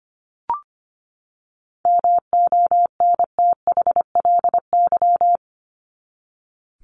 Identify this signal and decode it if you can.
Morse code, heard as a keyed tone, about 25 wpm, 700 Hz. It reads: MONTHLY
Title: LY Suffix - Top 100 Words with Base Words 25wpm